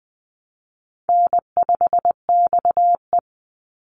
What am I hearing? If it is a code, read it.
Morse code, heard as a keyed tone, about 20 wpm, 700 Hz. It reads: N5XE
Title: Easy Call Signs - 20wpm